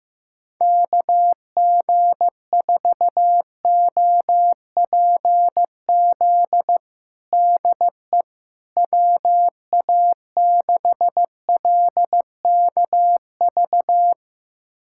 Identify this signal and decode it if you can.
Morse code, heard as a keyed tone, about 16 wpm, 700 Hz. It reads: KG4OPZ DE WA6LKV
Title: USA Callsign DE Callsign - 6 Characters - 15wpm